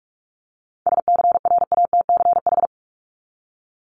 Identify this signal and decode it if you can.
Morse code, heard as a keyed tone, about 44 wpm, 700 Hz. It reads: SCRATCH
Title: Top 1501-1750 Words 45wpm